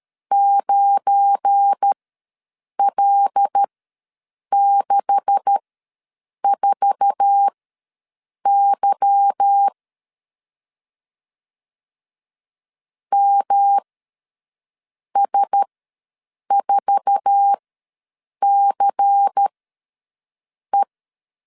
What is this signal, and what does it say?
Morse code, heard as a keyed tone, about 13 wpm, spaced as for 6 wpm, 800 Hz. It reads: 9L64Y MS4CE
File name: LEZIONI / 9 286 ws 12wpm 3351